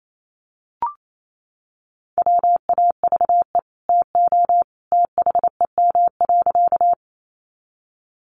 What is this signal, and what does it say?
Morse code, heard as a keyed tone, about 28 wpm, 700 Hz. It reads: WAVE TO THEM.
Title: Sentences from Top 1400 Words 28wpm